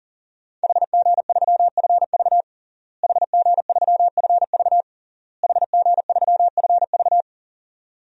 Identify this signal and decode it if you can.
Morse code, heard as a keyed tone, about 40 wpm, 700 Hz. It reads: HG3FV HG3FV HG3FV
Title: DXCC-100 Call Signs - 5 Characters - Repeated 3x - 40wpm